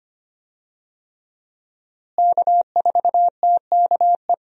Morse code, heard as a keyed tone, about 25 wpm, 700 Hz. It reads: K4TKE
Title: USA Call Signs - 5 Characters - Repeated 3x - 25wpm